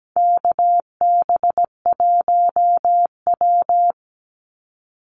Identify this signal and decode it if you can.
Morse code, heard as a keyed tone, about 17 wpm, 700 Hz. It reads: KB1W
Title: Call Signs Repeated 3x - 17wpm